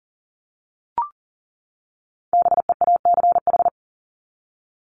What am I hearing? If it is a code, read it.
Morse code, heard as a keyed tone, about 40 wpm, 700 Hz. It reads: BEACH
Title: Top 1251-1500 Words 40wpm